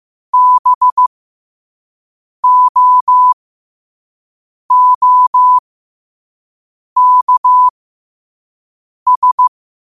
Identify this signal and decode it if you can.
Morse code, heard as a keyed tone, about 15 wpm, spaced as for 5 wpm, 1000 Hz. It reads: BOOKS
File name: qotdfiles / 5wpm